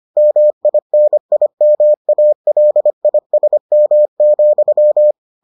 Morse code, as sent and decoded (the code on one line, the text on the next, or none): -- .. -. .. -- .- .-.. .. ... -- --..--
MINIMALISM,